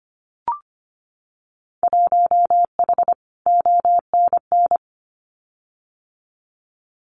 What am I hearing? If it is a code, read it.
Morse code, heard as a keyed tone, about 25 wpm, 700 Hz. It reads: 1H ONN
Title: Winter Field Day Exchanges 25wpm